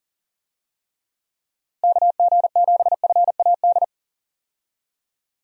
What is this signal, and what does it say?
Morse code, heard as a keyed tone, about 40 wpm, 700 Hz. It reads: KG7FAD
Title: USA Call Signs - 6 Characters - Repeated 2x - 40wpm